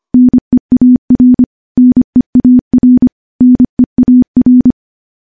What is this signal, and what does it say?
Morse code, heard as a keyed tone, about 25 wpm, 250 Hz. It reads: NEAR NEAR NEAR
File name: morse / morse-training-25WPM